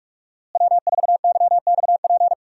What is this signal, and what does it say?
Morse code, heard as a keyed tone, about 46 wpm, 700 Hz. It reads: W4YXP
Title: USA Call Signs - 5 Characters - Repeated 3x - 45wpm